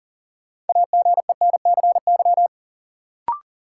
Morse code, heard as a keyed tone, about 40 wpm, 700 Hz. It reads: AGENCY
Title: Sentences from Top 1300 Words - New Word Drill - 40wpm